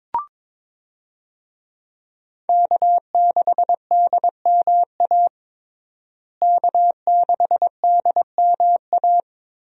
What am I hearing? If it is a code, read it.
Morse code, heard as a keyed tone, about 22 wpm, 700 Hz. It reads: K6DMA K6DMA
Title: USA Call Signs - 5 Characters - Repeated 2x - 22wpm